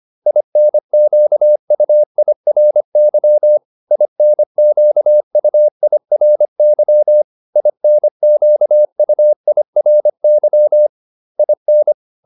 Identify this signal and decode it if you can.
Morse code, heard as a keyed tone, about 25 wpm, 600 Hz. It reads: INQUIRY INQUIRY INQUIRY IN